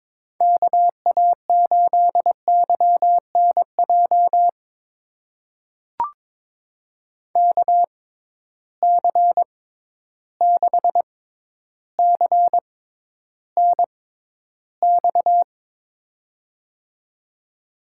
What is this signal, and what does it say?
Morse code, heard as a keyed tone, about 22 wpm, spaced as for 11 wpm, 700 Hz. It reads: KA8YNJ KC6CNX
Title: USA Call Signs - 6 Characters - 6x Character Spacing - 22wpm